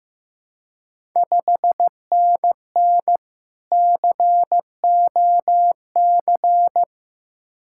Morse code, heard as a keyed tone, about 15 wpm, 690 Hz. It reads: 5NN COC
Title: Ontario QSO Party Exchange 15wpm